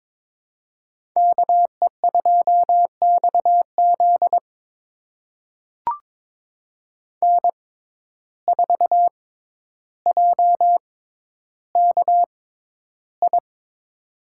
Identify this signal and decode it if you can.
Morse code, heard as a keyed tone, about 22 wpm, spaced as for 11 wpm, 700 Hz. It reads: KE2XZ N4JKI
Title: USA Call Signs - 5 Characters - 6x Character Spacing - 22wpm